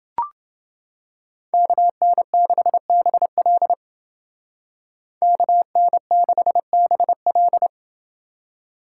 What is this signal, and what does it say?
Morse code, heard as a keyed tone, about 28 wpm, 700 Hz. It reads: KN6BL KN6BL